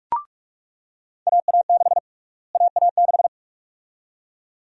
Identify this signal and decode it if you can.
Morse code, heard as a keyed tone, about 43 wpm, 700 Hz. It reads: AA6 AA6
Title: USA Call Sign Prefixes - Repeated 2x - 45wpm